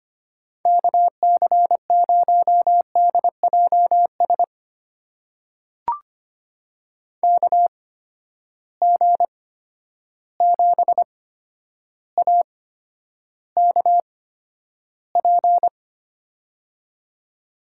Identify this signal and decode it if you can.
Morse code, heard as a keyed tone, about 25 wpm, spaced as for 11 wpm, 700 Hz. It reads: KC0DJS KG7AKP
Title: USA Call Signs - 6 Characters - 8x Character Spacing - 25wpm